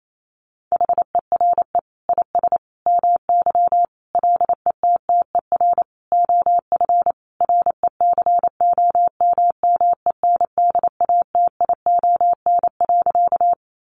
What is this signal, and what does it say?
Morse code, heard as a keyed tone, about 28 wpm, 700 Hz. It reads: HERE IS MY LETTER OF RECOMMENDATION.
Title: Sentences from Top 1400 Words 28wpm